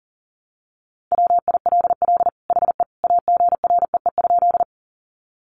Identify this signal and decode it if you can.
Morse code, heard as a keyed tone, about 40 wpm, 700 Hz. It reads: WILL HE AGREE?
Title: Sentences from Top 1100 Words 40wpm